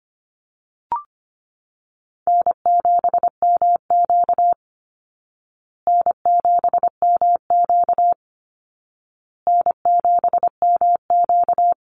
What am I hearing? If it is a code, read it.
Morse code, heard as a keyed tone, about 25 wpm, 700 Hz. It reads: N7MQ N7MQ N7MQ
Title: Easy Call Signs Repeated 3x - 25wpm